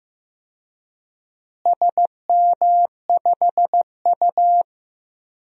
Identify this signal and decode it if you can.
Morse code, heard as a keyed tone, about 15 wpm, 700 Hz. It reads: SM5U
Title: DXCC-100 Call Signs - 4 Characters - Repeated 3x - 15wpm